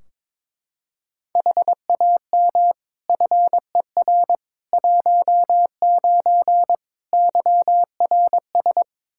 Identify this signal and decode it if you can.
Morse code, heard as a keyed tone, about 22 wpm, 700 Hz. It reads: HAM FER 19 YRS
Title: K1USN SST QSO Practice 22wpm